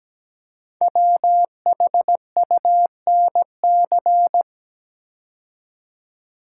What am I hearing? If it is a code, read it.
Morse code, heard as a keyed tone, about 17 wpm, 700 Hz. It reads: WHUNC
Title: Code Groups - 5 Letters - 17wpm